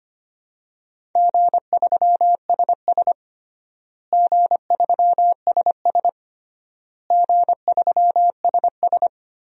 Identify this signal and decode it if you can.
Morse code, heard as a keyed tone, about 25 wpm, 700 Hz. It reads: G3SS G3SS G3SS
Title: DXCC-100 Call Signs - 4 Characters - Repeated 3x - 25wpm